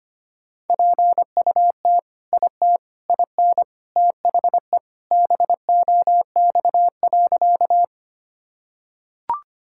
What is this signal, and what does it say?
Morse code, heard as a keyed tone, about 25 wpm, 700 Hz. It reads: PUT IT IN THE BOX.
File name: practice / 25wpm / Sentences with 2-Word Phrases - 25wpm